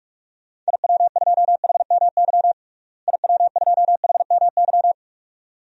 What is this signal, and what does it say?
Morse code, heard as a keyed tone, about 46 wpm, 700 Hz. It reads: IW2HMY IW2HMY
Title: DXCC-100 Call Signs - 6 Characters - Repeated 2x - 45wpm